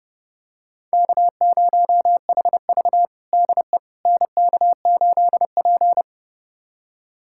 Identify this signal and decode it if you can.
Morse code, heard as a keyed tone, about 32 wpm, 700 Hz. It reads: K0HV DE NK8P
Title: USA Callsign DE Callsign - 4 Characters - 30wpm